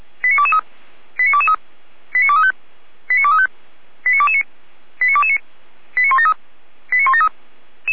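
Five-tone selective call (ZVEI), heard as quick runs of five tones, tones of about 70 ms. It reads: 89223 89223 89235 89235 89229 89229 89173 89173
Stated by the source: ZVEI five-tone selective call received on SDR.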